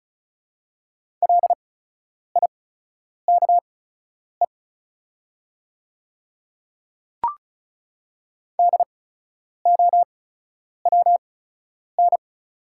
Morse code, heard as a keyed tone, about 34 wpm, spaced as for 10 wpm, 690 Hz. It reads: LIKE DOWN
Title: 8x Character Spacing - Top 100 Words - 35wpm